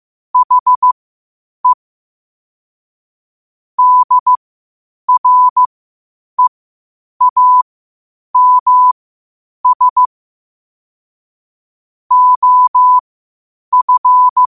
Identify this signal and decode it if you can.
Morse code, heard as a keyed tone, about 15 wpm, spaced as for 8 wpm, 1000 Hz. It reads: HE DREAMS OF
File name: qotdfiles / 8wpm